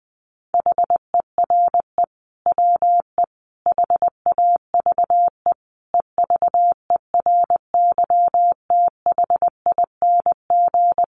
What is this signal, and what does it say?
Morse code, heard as a keyed tone, about 19 wpm, 700 Hz. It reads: HERE WE HAVE EVERYTHING
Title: Sentences from Top 1100 Words 20wpm